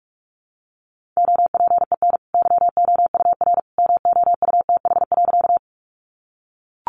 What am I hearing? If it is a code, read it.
Morse code, heard as a keyed tone, about 44 wpm, 700 Hz. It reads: OPEN YOUR MOUTH.
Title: Sentences from Top 1500 Words 45wpm